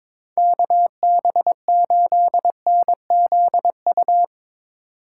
Morse code, heard as a keyed tone, about 22 wpm, 700 Hz. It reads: KB8NZU
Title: USA Call Signs - 6 Characters - 2x Character Spacing - 22wpm